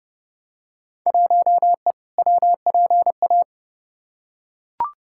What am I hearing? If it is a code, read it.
Morse code, heard as a keyed tone, about 30 wpm, 700 Hz. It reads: 1E WPA
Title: Field Day Exchanges - 30wpm